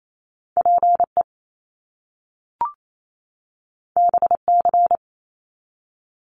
Morse code, heard as a keyed tone, about 28 wpm, 700 Hz. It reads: PE BC
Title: Canadian Provinces and Territories 28wpm